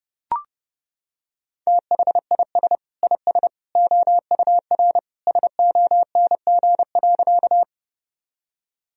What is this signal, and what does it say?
Morse code, heard as a keyed tone, about 30 wpm, 700 Hz. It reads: THIS IS OUR SONG.
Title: Sentences from Top 1000 Words - Review Part 2 - 30wpm